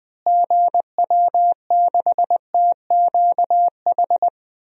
Morse code, heard as a keyed tone, about 21 wpm, 700 Hz. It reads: GW6TQH
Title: DXCC-100 Call Signs - 6 Characters - Repeated 2x - 20wpm